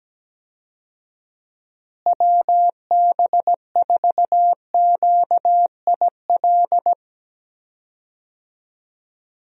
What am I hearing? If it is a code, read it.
Morse code, heard as a keyed tone, about 17 wpm, 700 Hz. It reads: WB4QIL
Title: USA Call Signs - 6 Characters - Repeated 3x - 17wpm